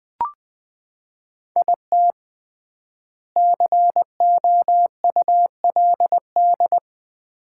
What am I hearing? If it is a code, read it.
Morse code, heard as a keyed tone, about 21 wpm, 700 Hz. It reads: IT COULD
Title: 3x Word-Spacing - Sets of 3 Words - 20wpm